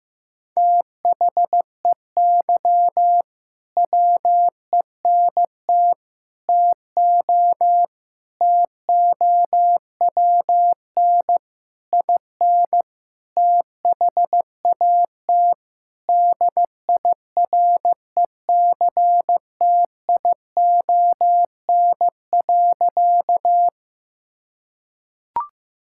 Morse code, heard as a keyed tone, about 16 wpm, 700 Hz. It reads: THEY WENT TO TOWN IN THAT DIRECTION.
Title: Sentences from Top 1000 Words - Review Part 5 - 15wpm